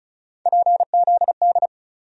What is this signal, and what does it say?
Morse code, heard as a keyed tone, about 34 wpm, 700 Hz. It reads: PZD
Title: USA Call Sign Postfixes - 3 Letters 35wpm